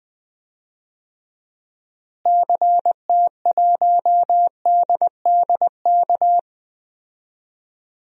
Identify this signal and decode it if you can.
Morse code, heard as a keyed tone, about 20 wpm, 700 Hz. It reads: CT1DDK
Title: DXCC-100 Call Signs - 6 Characters - Repeated 3x - 20wpm